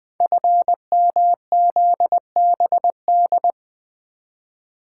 Morse code, heard as a keyed tone, about 20 wpm, 700 Hz. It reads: FMZBD
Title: Code Groups - 5 Letters - 20wpm